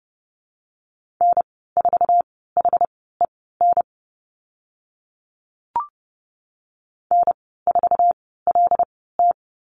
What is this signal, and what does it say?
Morse code, heard as a keyed tone, about 30 wpm, spaced as for 17 wpm, 700 Hz. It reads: N4HEN N4LT
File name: Lesson 15g - Number 5 - Call Signs 30wpm